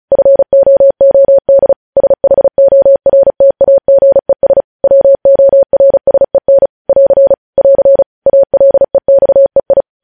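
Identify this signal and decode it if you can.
Morse code, heard as a keyed tone, about 36 wpm, 550 Hz. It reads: FOOD SHORTAGES WORSEN <AR> <AR> ALEXEI
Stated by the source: News headlines sent in Morse at speed 35.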